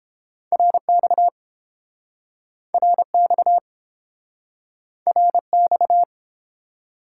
Morse code, heard as a keyed tone, about 29 wpm, 700 Hz. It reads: RX RX RX